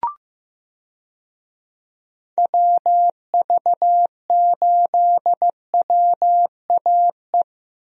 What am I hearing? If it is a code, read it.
Morse code, heard as a keyed tone, about 15 wpm, 700 Hz. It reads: WV8WAE
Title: USA Call Signs - 6 Characters - Repeated 2x - 15wpm